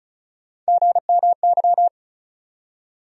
Morse code, heard as a keyed tone, about 36 wpm, 700 Hz. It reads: GMY
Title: USA Call Sign Postfixes - 3 Letters 35wpm